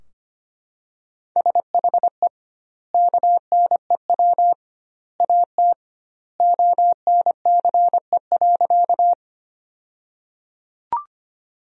Morse code, heard as a keyed tone, about 25 wpm, 700 Hz. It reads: SHE KNEW AT ONCE.
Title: 2x Word-Spacing - Sentences from Top 400 Words - 25wpm